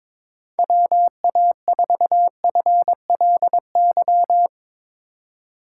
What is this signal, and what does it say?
Morse code, heard as a keyed tone, about 22 wpm, 700 Hz. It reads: WA4FLY
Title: USA Call Signs - 2x Character Spacing - 22wpm